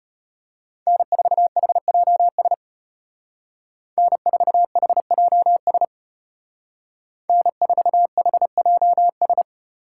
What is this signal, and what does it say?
Morse code, heard as a keyed tone, about 30 wpm, 700 Hz. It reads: N4HJS N4HJS N4HJS
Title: Speed Racing - USA 1x3 Call Signs 25wpm